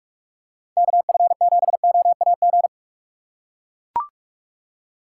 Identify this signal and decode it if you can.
Morse code, heard as a keyed tone, about 44 wpm, 690 Hz. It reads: KF7OAG
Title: USA Call Signs - 6 Characters - 2x Character Spacing - 45wpm